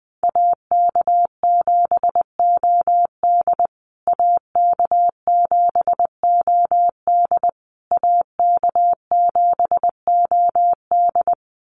Morse code, heard as a keyed tone, about 20 wpm, 700 Hz. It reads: AK7OD AK7OD AK7OD
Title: USA Call Signs Repeated 3x 20wpm